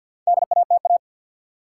Morse code, heard as a keyed tone, about 50 wpm, 700 Hz. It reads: DATA
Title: Sentences from Top 1000 Words - Review Part 3 - New Word Drill - 50wpm